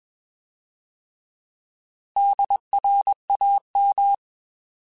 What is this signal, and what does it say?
Morse code, heard as a keyed tone, about 21 wpm, 800 Hz. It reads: DRAM
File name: Alphabetical / Alphabetical mp3s / Ds @20